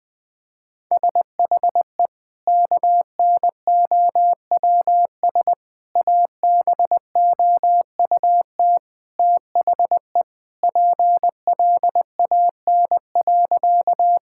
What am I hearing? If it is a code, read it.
Morse code, heard as a keyed tone, about 21 wpm, 700 Hz. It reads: SHE KNOWS ABOUT THE PLAN.
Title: Sentences with 2-Word Phrases - Spoken Beginning - 20wpm